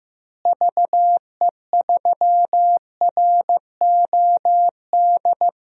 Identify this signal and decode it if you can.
Morse code, heard as a keyed tone, about 15 wpm, 700 Hz. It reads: VE3ROD
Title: DXCC-100 Call Signs - 6 Characters - Repeated 3x - 15wpm